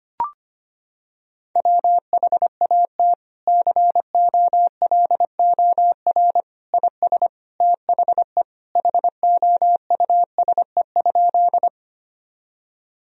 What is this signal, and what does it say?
Morse code, heard as a keyed tone, about 25 wpm, 700 Hz. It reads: WHAT COLOR IS THE HOUSE?
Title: Sentences from Top 1000 Words - Review Part 2 - 25wpm